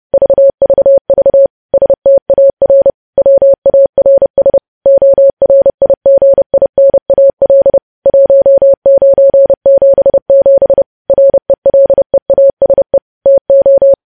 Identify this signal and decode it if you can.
Morse code, heard as a keyed tone, about 30 wpm, 550 Hz. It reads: VVV STAR WARS ORIGINAL 1977 RELEASE TO
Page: news headlines in morse code at 30 wpm